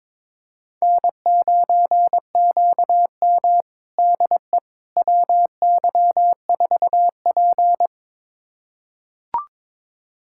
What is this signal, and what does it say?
Morse code, heard as a keyed tone, about 22 wpm, 700 Hz. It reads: N9QM DE WY4P
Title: USA Callsign DE Callsign - 4 Characters - 22wpm